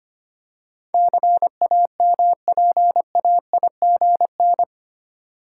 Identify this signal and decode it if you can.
Morse code, heard as a keyed tone, about 25 wpm, 700 Hz. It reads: CAMPAIGN
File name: Eight Letter Words - 25wpm